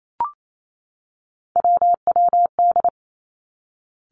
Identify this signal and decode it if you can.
Morse code, heard as a keyed tone, about 28 wpm, 700 Hz. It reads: WWD
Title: Three Challenging Letter-Number Combinations 28wpm